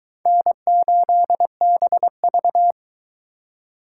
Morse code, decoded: N8BV

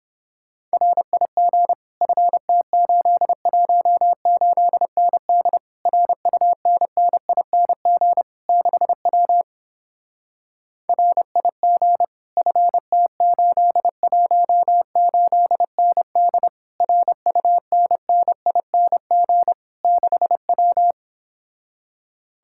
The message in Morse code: RIG FT818ND RUNNING 6W RIG FT818ND RUNNING 6W

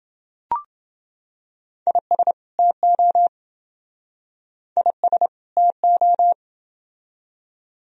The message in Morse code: IS TO IS TO

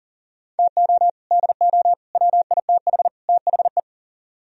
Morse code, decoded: TO DO WITH THE

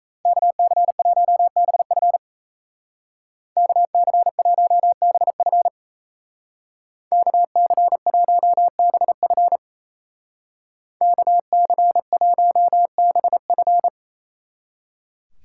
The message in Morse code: KC1BF KC1BF KC1BF KC1BF

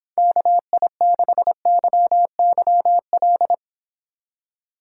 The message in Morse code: KI6YYL

